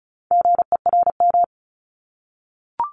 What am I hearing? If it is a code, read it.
Morse code, heard as a keyed tone, about 36 wpm, 700 Hz. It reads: GERM
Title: Multi-letter Phonemes Repeated 3x 35wpm